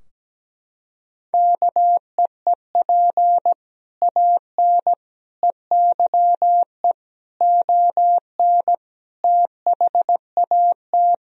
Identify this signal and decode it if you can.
Morse code, heard as a keyed tone, about 17 wpm, 700 Hz. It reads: KEEP AN EYE ON THAT
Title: Sentences from Top 1000 Words - Review Part 2 - 17wpm